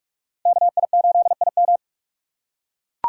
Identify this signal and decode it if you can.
Morse code, heard as a keyed tone, about 44 wpm, 700 Hz. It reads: KI8IM
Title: USA Call Signs - 5 Characters - 4x Character Spacing - 45wpm